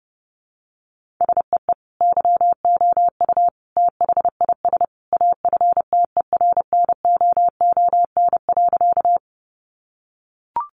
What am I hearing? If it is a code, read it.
Morse code, heard as a keyed tone, about 30 wpm, 700 Hz. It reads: SEE YOU THIS AFTERNOON.
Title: Sentences from Top 1500 Words 30wpm